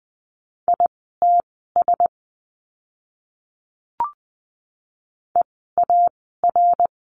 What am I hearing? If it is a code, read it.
Morse code, heard as a keyed tone, about 20 wpm, spaced as for 14 wpm, 700 Hz. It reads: ITS EAR
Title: 054 Lesson 10f - Letter r - Words 20wpm